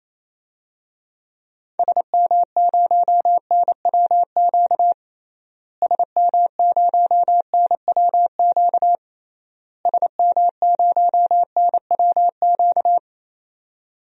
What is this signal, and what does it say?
Morse code, heard as a keyed tone, about 28 wpm, 700 Hz. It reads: SM0NWQ SM0NWQ SM0NWQ
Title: DXCC-100 Call Signs - 6 Characters - Repeated 3x - 28wpm